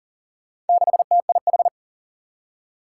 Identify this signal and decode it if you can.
Morse code, heard as a keyed tone, about 39 wpm, 700 Hz. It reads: 6TIH
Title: Four Challenging Letter-Number Combinations 40wpm